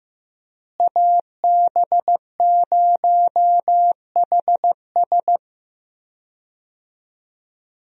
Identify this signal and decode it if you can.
Morse code, heard as a keyed tone, about 15 wpm, 700 Hz. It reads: AB0HS
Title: USA Call Signs - 5 Characters - Repeated 2x - 15wpm